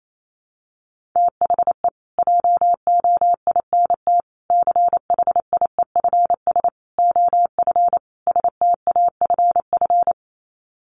Morse code, heard as a keyed tone, about 28 wpm, 700 Hz. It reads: THE JOINT CHIEFS OF STAFF
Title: Sets of 5 Words Encore2 28wpm